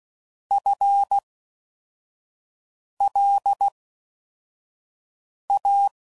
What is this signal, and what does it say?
Morse code, heard as a keyed tone, about 16 wpm, spaced as for 4 wpm, 800 Hz. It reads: FLA